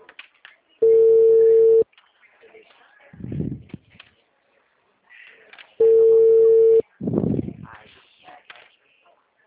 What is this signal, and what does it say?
Ringback tone (China, 450 Hz), with speech.
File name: out- -2013- - - .95312